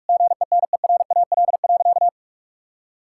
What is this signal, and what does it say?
Morse code, heard as a keyed tone, about 46 wpm, 700 Hz. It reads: GENERAL.
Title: Sentences from Top 1600 Words 45wpm